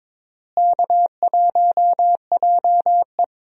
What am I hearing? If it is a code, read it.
Morse code, heard as a keyed tone, about 21 wpm, 700 Hz. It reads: K1JE
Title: USA Call Signs - 4 Characters - Repeated 3x - 22wpm